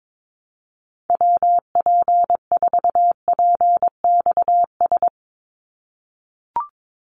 Morse code, heard as a keyed tone, about 22 wpm, 700 Hz. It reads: WP4PXS
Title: USA Call Signs 22wpm